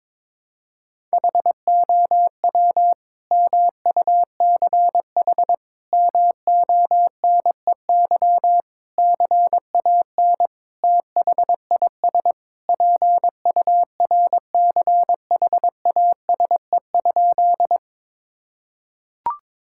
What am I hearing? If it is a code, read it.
Morse code, heard as a keyed tone, about 21 wpm, 700 Hz. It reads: HOW MUCH MONEY CAN THIS PURCHASE?
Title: Sentences from Top 1000 Words - Review Part 5 - 22wpm